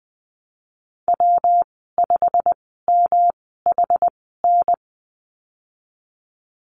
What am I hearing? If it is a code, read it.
Morse code, heard as a keyed tone, about 20 wpm, spaced as for 15 wpm, 700 Hz. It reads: W5MHN
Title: 173 Lesson 28g - Number 9 - Call Signs 20wpm